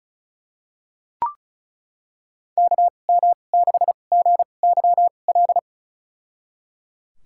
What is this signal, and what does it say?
Morse code, heard as a keyed tone, about 36 wpm, spaced as for 25 wpm, 700 Hz. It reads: KM6GYL
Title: USA Call Signs - 6 Characters - 2x Character Spacing - 35wpm